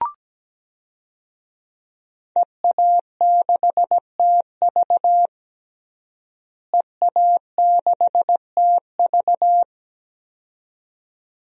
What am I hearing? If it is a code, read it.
Morse code, heard as a keyed tone, about 17 wpm, 700 Hz. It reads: EA6TV EA6TV
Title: DXCC-100 Call Signs - 5 Characters - Repeated 2x - 17wpm